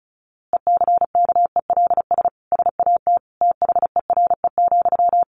Morse code, heard as a keyed tone, about 34 wpm, 700 Hz. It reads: ECKELS SAT THERE,
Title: A Sound Of Thunder 35wpm